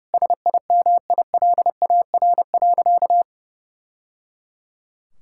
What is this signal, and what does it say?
Morse code, heard as a keyed tone, about 30 wpm, 700 Hz. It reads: SIMILAR.
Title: Sentences from Top 1000 Words - Review Part 5 - 30wpm